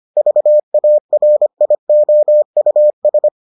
Morse code, heard as a keyed tone, about 25 wpm, 600 Hz. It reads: VARIOUS